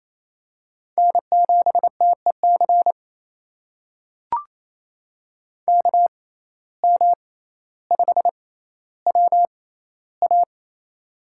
Morse code, heard as a keyed tone, about 28 wpm, spaced as for 14 wpm, 700 Hz. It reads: N7TEC KM5WA
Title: USA Call Signs - 5 Characters - 6x Character Spacing - 28wpm